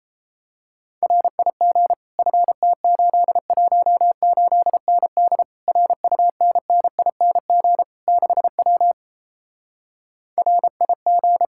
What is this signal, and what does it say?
Morse code, heard as a keyed tone, about 32 wpm, 700 Hz. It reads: RIG FT818ND RUNNING 6W RIG
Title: Speed-Racing Common QSO Phrases - 28wpm